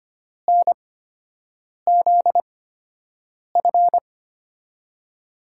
Morse code, heard as a keyed tone, about 25 wpm, spaced as for 7 wpm, 700 Hz. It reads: NZF
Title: USA Call Signs - 8x Character Spacing - 25wpm